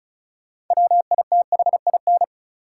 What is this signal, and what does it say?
Morse code, heard as a keyed tone, about 34 wpm, 700 Hz. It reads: WITHIN
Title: Sentences from Top 700 Words - New Word Drill - 35wpm